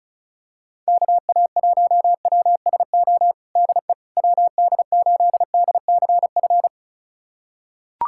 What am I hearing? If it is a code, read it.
Morse code, heard as a keyed tone, about 34 wpm, 690 Hz. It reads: KA1WSO DE WD8DCF
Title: USA Callsign DE Callsign - 6 Characters - 35wpm